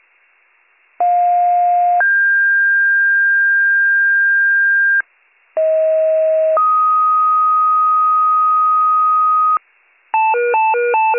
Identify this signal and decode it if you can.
Two-tone page, heard as a one-second tone, then a three-second tone, about 710 then 1600 Hz, about 640 then 1250 Hz, with noise. It is heard from 1.0 to 9.6 s.